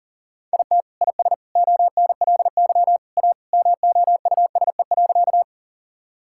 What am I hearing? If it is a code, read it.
Morse code, heard as a keyed tone, about 40 wpm, 700 Hz. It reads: IT IS ONLY A MOUSE.
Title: Sentences from Top 1000 Words - Review Part 5 - 40wpm